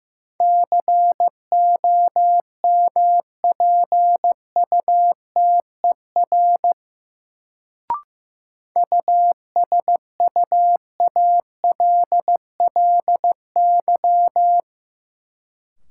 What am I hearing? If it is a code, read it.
Morse code, heard as a keyed tone, about 15 wpm, 700 Hz. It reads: COMPUTER USUALLY
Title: Sentences from Top 1000 Words - Review Part 3 - New Word Drill - 15wpm